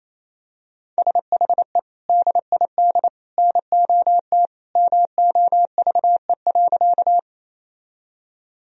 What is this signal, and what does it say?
Morse code, heard as a keyed tone, about 28 wpm, 700 Hz. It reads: SHE DID NOT MOVE.
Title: Sentences from Top 1000 Words - Review Part 1 - 28wpm